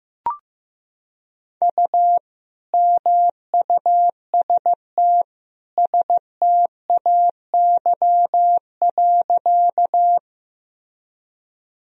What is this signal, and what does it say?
Morse code, heard as a keyed tone, about 15 wpm, 700 Hz. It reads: U MUST STAY.
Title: Sentences from Top 500 Words and Abbreviations 15wpm